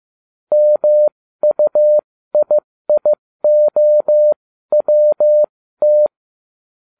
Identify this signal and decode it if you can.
Morse code, heard as a keyed tone, about 15 wpm, spaced as for 13 wpm, 600 Hz. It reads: MUIIOWT